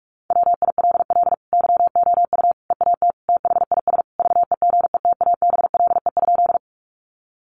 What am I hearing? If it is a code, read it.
Morse code, heard as a keyed tone, about 44 wpm, 700 Hz. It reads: WILL YOU EAT THIS VEGETABLE?
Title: Sentences from Top 1300 Words 45wpm